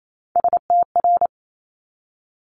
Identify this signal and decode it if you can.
Morse code, heard as a keyed tone, about 28 wpm, 700 Hz. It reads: STR
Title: Multi-letter Phonemes 28wpm